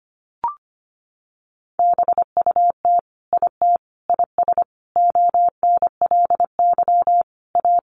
Morse code, heard as a keyed tone, about 25 wpm, 700 Hz. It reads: BUT IT IS ONLY A
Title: Sentences from Top 1500 Words 25wpm